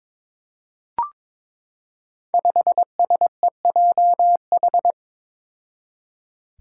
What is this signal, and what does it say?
Morse code, heard as a keyed tone, about 22 wpm, 700 Hz. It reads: 5SEJH
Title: Five Challenging Letter-Number Combinations 22wpm